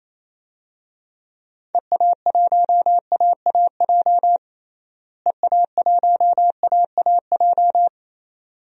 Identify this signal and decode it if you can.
Morse code, heard as a keyed tone, about 28 wpm, 700 Hz. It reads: EA1AAJ EA1AAJ
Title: DXCC-100 Call Signs - 6 Characters - Repeated 2x - 28wpm